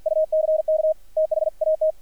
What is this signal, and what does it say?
Morse code, heard as a keyed tone, about 53 wpm, 590 Hz, with radio noise. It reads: UCK THAT
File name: archived / CRSnet